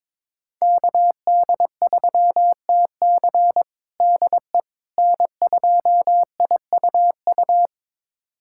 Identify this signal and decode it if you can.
Morse code, heard as a keyed tone, about 22 wpm, 700 Hz. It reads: KD3TC DE N2IUU
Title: USA Callsign DE Callsign - 5 Characters - 22wpm